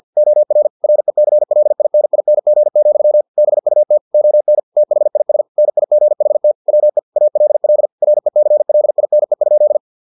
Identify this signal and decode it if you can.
Morse code, heard as a keyed tone, about 50 wpm, 600 Hz. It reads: OR RECLINING, BUT ON THIS NIGHT WE ALL RECLINE?